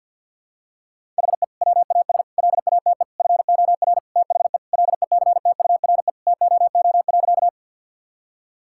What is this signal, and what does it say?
Morse code, heard as a keyed tone, about 50 wpm, 700 Hz. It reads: HE WAS LATE FOR THE LECTURE TOO.
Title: Sentences from Top 1600 Words 50wpm